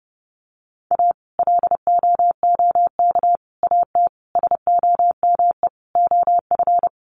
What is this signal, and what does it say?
Morse code, heard as a keyed tone, about 30 wpm, 700 Hz. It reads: A LOOK AT SOME OF
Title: Sets of 5 Words Encore2 30wpm